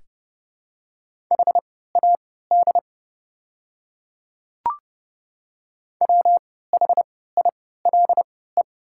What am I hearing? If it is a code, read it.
Morse code, heard as a keyed tone, about 30 wpm, spaced as for 17 wpm, 700 Hz. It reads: HAD WHILE
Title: 3x Character Spacing - Top 300 Words - 30wpm